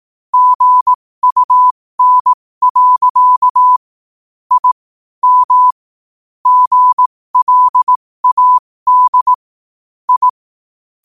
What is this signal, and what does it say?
Morse code, heard as a keyed tone, about 18 wpm, spaced as for 15 wpm, 1000 Hz. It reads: GUN. I M GLAD I
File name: qotdfiles / 15wpm